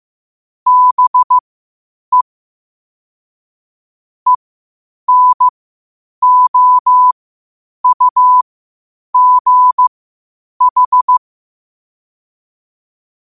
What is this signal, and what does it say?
Morse code, heard as a keyed tone, about 15 wpm, spaced as for 8 wpm, 1000 Hz. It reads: BE ENOUGH
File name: qotdfiles / 8wpm